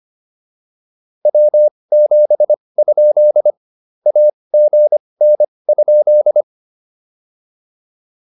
Morse code, heard as a keyed tone, about 25 wpm, spaced as for 20 wpm, 600 Hz. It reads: W7? AGN?